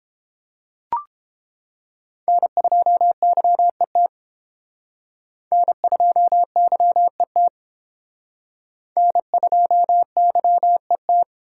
Morse code, heard as a keyed tone, about 28 wpm, 700 Hz. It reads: N2YET N2YET N2YET